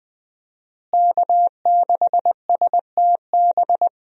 Morse code, heard as a keyed tone, about 20 wpm, 700 Hz. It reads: K6STB
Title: USA Call Signs - 5 Characters - Repeated 3x - 20wpm